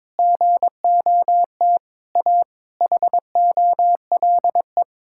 Morse code, heard as a keyed tone, about 22 wpm, 700 Hz. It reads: GOT A HOLE
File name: All-Age Appropriate Jokes - 22wpm